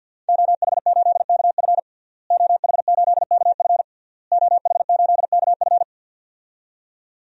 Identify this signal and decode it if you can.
Morse code, heard as a keyed tone, about 50 wpm, 700 Hz. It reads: OH8KF OH8KF OH8KF
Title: DXCC-100 Call Signs - 5 Characters - Repeated 3x - 50wpm